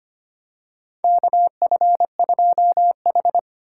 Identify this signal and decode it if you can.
Morse code, heard as a keyed tone, about 25 wpm, 700 Hz. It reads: KF2H
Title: USA Call Signs - 4 Characters - 3x Character Spacing - 25wpm